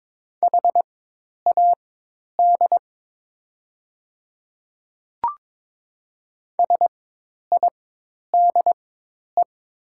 Morse code, heard as a keyed tone, about 22 wpm, spaced as for 10 wpm, 700 Hz. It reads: HAD SIDE
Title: 4x Character Spacing - Top 100 Words - 22wpm